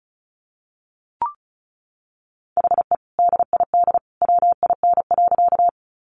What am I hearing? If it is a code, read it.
Morse code, heard as a keyed tone, about 34 wpm, 700 Hz. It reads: HE DID WIN.
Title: Sentences from Top 1500 Words 35wpm